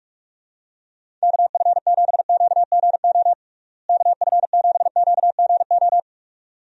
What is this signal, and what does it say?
Morse code, heard as a keyed tone, about 46 wpm, 700 Hz. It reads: KF7QGO KF7QGO
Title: USA Call Signs - 6 Characters - Repeated 2x - 45wpm